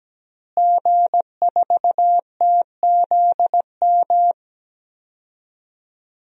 Morse code, heard as a keyed tone, about 17 wpm, 700 Hz. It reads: G4TZM
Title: DXCC-100 Call Signs - 5 Characters - Repeated 2x - 17wpm